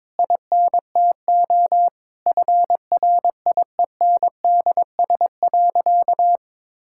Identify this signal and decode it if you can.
Morse code, heard as a keyed tone, about 21 wpm, 700 Hz. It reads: INTO FRIENDS.
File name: Difficult Sentences with Hints - 22wpm